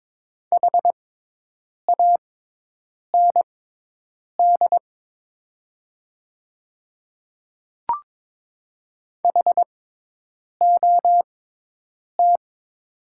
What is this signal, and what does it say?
Morse code, heard as a keyed tone, about 22 wpm, spaced as for 8 wpm, 700 Hz. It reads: HAND HOT
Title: 6x Character Spacing - Top 300 Words - 22wpm